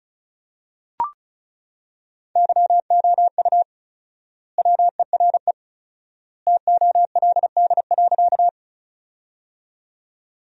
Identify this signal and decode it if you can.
Morse code, heard as a keyed tone, about 36 wpm, 700 Hz. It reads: YOU WERE TOLD.
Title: 4x Word-Spacing - Sentences from Top 400 Words - 35wpm